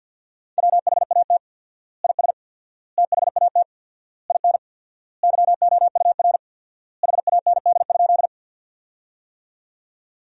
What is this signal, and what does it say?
Morse code, heard as a keyed tone, about 53 wpm, 700 Hz. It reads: WHAT IS THAT IN YOUR HAND?
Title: 4x Word-Spacing - Sentences from Top 200 Words - 50wpm